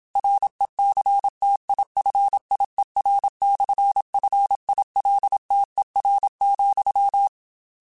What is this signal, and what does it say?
Morse code, heard as a keyed tone, about 27 wpm, 800 Hz. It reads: RECTIFIER/FILTER,
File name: Technical Subjects / Technical Subjects-mp3s / Sw Pwr Sup @25